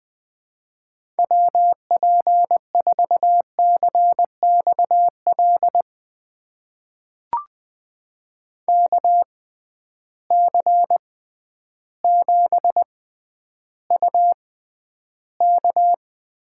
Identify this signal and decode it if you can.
Morse code, heard as a keyed tone, about 20 wpm, spaced as for 11 wpm, 700 Hz. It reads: WP4CXL KC7UK
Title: USA Call Signs - 6 Characters - 6x Character Spacing - 20wpm